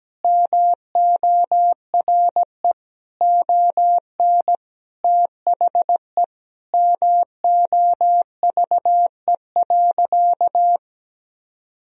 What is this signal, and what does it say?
Morse code, heard as a keyed tone, about 17 wpm, 700 Hz. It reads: MORE ON THE MOVE.
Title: Sentences from Top 1000 Words - Review Part 3 - 17wpm